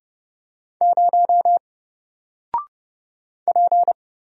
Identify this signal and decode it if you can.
Morse code, heard as a keyed tone, about 30 wpm, 700 Hz. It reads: 0 P
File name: Single-Letter-Number-30wpm